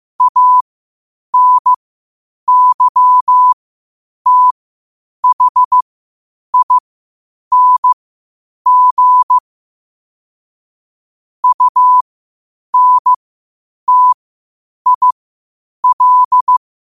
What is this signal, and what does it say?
Morse code, heard as a keyed tone, about 15 wpm, spaced as for 8 wpm, 1000 Hz. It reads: ANYTHING UNTIL
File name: qotdfiles / 8wpm